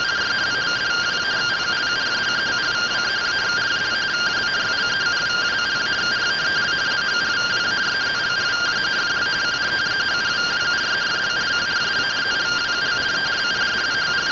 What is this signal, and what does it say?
Two-tone teleprinter signal, about 170 Hz shift.